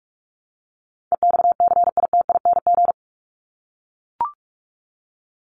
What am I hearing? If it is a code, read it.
Morse code, heard as a keyed tone, about 44 wpm, 700 Hz. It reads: EXCITING
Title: Top 1751-2000 Words 45wpm